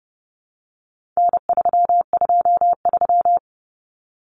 Morse code, heard as a keyed tone, about 30 wpm, 700 Hz. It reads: N323
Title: Common Cut Numbers - 4-Digit 30wpm